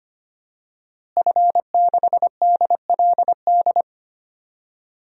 Morse code, heard as a keyed tone, about 25 wpm, 700 Hz. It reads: F6DLD
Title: DXCC-100 Call Signs - 5 Characters - Repeated 3x - 25wpm